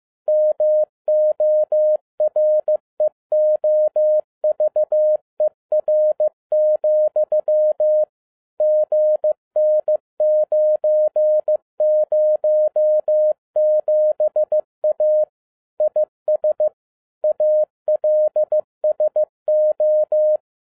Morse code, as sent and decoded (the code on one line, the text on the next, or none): -- --- .-. --- ...- . .-. --..-- / --. -. ----. ----- --... .- / .. ... / .- .-.. ... ---
MOREOVER, GN907A IS ALSO